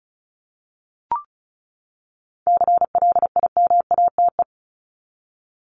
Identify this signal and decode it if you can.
Morse code, heard as a keyed tone, about 34 wpm, 700 Hz. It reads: CLIMATE
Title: Top 1501-1750 Words 35wpm